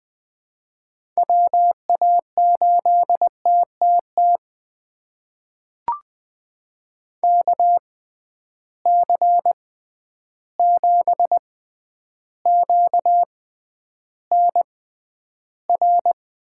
Morse code, heard as a keyed tone, about 21 wpm, spaced as for 10 wpm, 700 Hz. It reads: WA8TTT KC7QNR
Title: USA Call Signs - 6 Characters - 6x Character Spacing - 20wpm